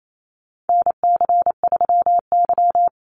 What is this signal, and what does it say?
Morse code, heard as a keyed tone, about 28 wpm, 700 Hz. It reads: NC3Y
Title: Call Signs - 28wpm